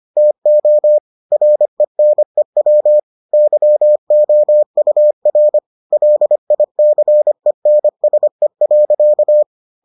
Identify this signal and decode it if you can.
Morse code, heard as a keyed tone, about 25 wpm, 600 Hz. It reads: TO RENEW YOUR LICENSE.